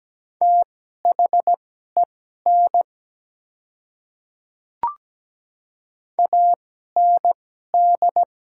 Morse code, heard as a keyed tone, about 17 wpm, spaced as for 12 wpm, 700 Hz. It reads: THEN AND
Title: 2x Character Spacing - Top 100 Words - 17wpm